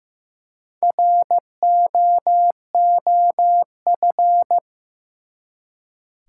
Morse code, heard as a keyed tone, about 15 wpm, 700 Hz. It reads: ROOF
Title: Four Letter Words - 15wpm